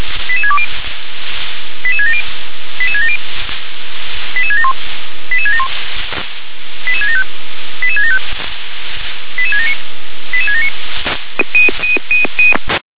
Five-tone selective call (ZVEI), heard as quick runs of five tones, tones of about 70 ms. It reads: 88620 88670 88670 88671 88671 88675 88675 88679 88679
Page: ZVEI